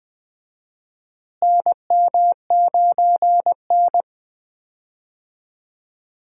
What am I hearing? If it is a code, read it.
Morse code, heard as a keyed tone, about 20 wpm, 700 Hz. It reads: NM9N
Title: USA Call Signs - 4 Characters - Repeated 2x - 20wpm